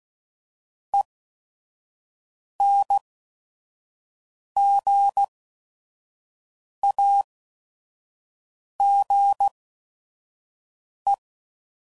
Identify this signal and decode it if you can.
Morse code, heard as a keyed tone, about 16 wpm, spaced as for 5 wpm, 800 Hz. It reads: ENGAGE
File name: Prefixes / By First Letter / Es